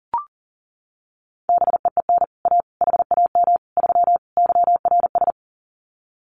Morse code, heard as a keyed tone, about 40 wpm, 700 Hz. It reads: BEEN A HAM 3 YRS
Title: Common QSO Phrases Encore 40wpm